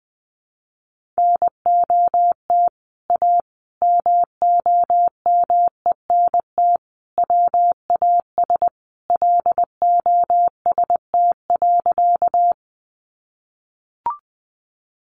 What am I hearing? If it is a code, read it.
Morse code, heard as a keyed tone, about 20 wpm, 700 Hz. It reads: NOT A MOMENT WAS LOST.
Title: Sentences from Top 1400 Words 20wpm